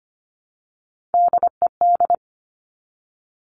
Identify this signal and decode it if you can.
Morse code, heard as a keyed tone, about 25 wpm, 700 Hz. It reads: DED
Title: Top Three Letter Combinations in Words 25wpm